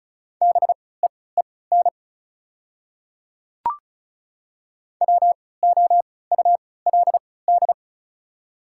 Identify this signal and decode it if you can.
Morse code, heard as a keyed tone, about 34 wpm, spaced as for 20 wpm, 690 Hz. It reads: BEEN WOULD
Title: 3x Character Spacing - Top 200 Words - 35wpm